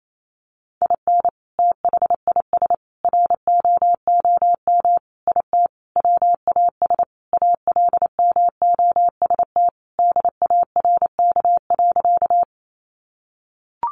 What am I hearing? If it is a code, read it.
Morse code, heard as a keyed tone, about 28 wpm, 700 Hz. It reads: IN THIS ROOM IT WAS ALMOST DARK.
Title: Sentences from Top 700 Words 28wpm